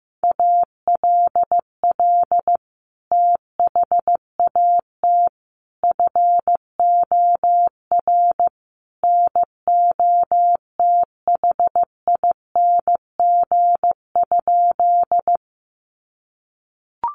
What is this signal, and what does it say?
Morse code, heard as a keyed tone, about 15 wpm, 700 Hz. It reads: ALL THAT FOR NOTHING?
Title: Sentences from Top 500 Words 15wpm